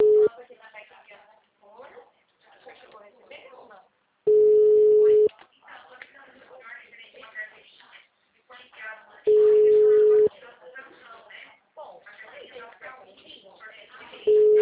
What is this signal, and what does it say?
Ringback tone (Europe, 425 Hz), with speech.